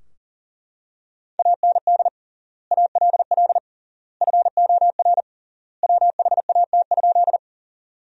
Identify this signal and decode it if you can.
Morse code, heard as a keyed tone, about 40 wpm, 700 Hz. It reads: AND ALL FOR WHAT?
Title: 3x Word-Spacing - Sentences from Top 100 Words - 40wpm